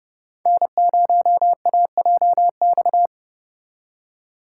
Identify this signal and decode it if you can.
Morse code, heard as a keyed tone, about 29 wpm, 700 Hz. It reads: N0AJX